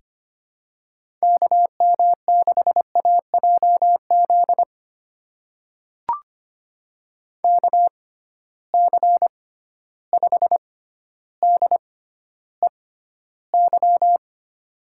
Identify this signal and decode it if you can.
Morse code, heard as a keyed tone, about 25 wpm, spaced as for 13 wpm, 700 Hz. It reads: KM6AJZ KC5DEY
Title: USA Call Signs - 6 Characters - 6x Character Spacing - 25wpm